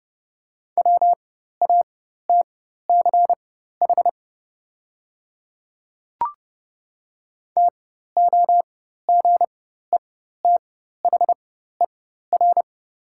Morse code, heard as a keyed tone, about 30 wpm, spaced as for 14 wpm, 700 Hz. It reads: WATCH TOGETHER
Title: 4x Character Spacing - Top 300 Words - 30wpm